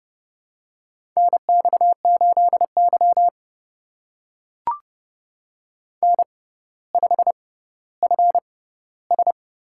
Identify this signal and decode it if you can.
Morse code, heard as a keyed tone, about 30 wpm, spaced as for 15 wpm, 700 Hz. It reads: NX8Y N5FS
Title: USA Call Signs - 4 Characters - 6x Character Spacing - 30wpm